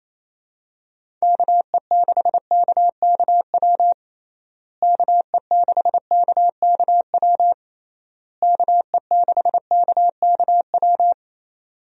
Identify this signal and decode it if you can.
Morse code, heard as a keyed tone, about 28 wpm, 700 Hz. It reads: KE6KKW KE6KKW KE6KKW
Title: USA Call Signs - 6 Characters - Repeated 3x - 28wpm